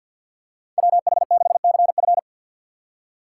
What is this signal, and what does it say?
Morse code, heard as a keyed tone, about 52 wpm, 700 Hz. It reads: WH6CF
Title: USA Call Signs - 3x Character Spacing - 50wpm